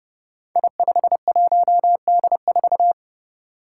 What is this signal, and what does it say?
Morse code, heard as a keyed tone, about 30 wpm, 700 Hz. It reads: I51D4
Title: Code Groups - 5 Letter:Numbers - 30wpm